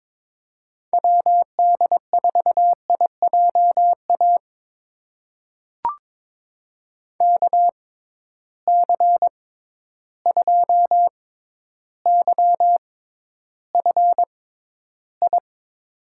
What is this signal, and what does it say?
Morse code, heard as a keyed tone, about 21 wpm, spaced as for 11 wpm, 700 Hz. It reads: WD4IJA KC2YFI
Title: USA Call Signs - 6 Characters - 6x Character Spacing - 22wpm